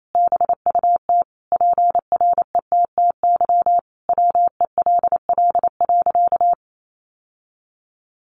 Morse code, heard as a keyed tone, about 28 wpm, 700 Hz. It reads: BUT PRETTY WELL.
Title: Sentences from Top 1000 Words 28wpm